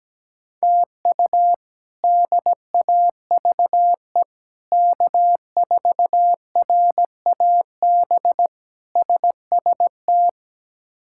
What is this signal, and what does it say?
Morse code, heard as a keyed tone, about 17 wpm, 700 Hz. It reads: TU DAVE K4RAB SST
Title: K1USN SST QSO Practice - 17wpm